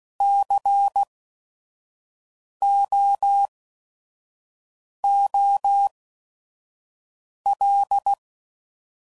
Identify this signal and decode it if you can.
Morse code, heard as a keyed tone, about 16 wpm, spaced as for 5 wpm, 800 Hz. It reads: COOL
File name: Prefixes / By First Letter / Cs